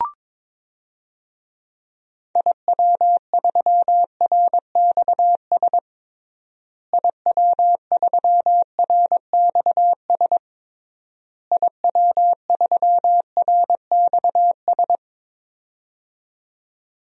Morse code, heard as a keyed tone, about 22 wpm, 700 Hz. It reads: IW3RXS IW3RXS IW3RXS
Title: DXCC-100 Call Signs - 6 Characters - Repeated 3x - 22wpm